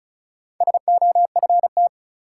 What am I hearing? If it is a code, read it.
Morse code, heard as a keyed tone, about 34 wpm, 700 Hz. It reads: SOFT
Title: Sentences from Top 1500 Words - New Word Drill Alt - 35wpm